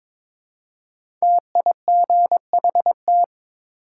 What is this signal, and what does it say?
Morse code, heard as a keyed tone, about 22 wpm, 700 Hz. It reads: TIGHT
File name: 5 Letter Words - First 3 Letters Given - 22wpm